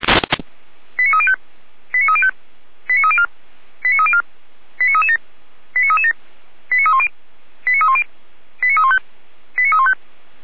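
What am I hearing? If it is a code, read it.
Five-tone selective call (ZVEI), heard as quick runs of five tones, tones of about 70 ms. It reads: 89305 89305 89304 89304 89337 89337 89310 89310 89315 89315
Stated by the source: ZVEI five-tone call received by SDR, from feuerwehr-orlamuende.org (fire brigade).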